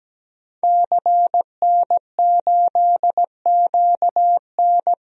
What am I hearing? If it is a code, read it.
Morse code, heard as a keyed tone, about 17 wpm, 700 Hz. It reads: CN8QN
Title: DXCC-100 Call Signs - 5 Characters - 17wpm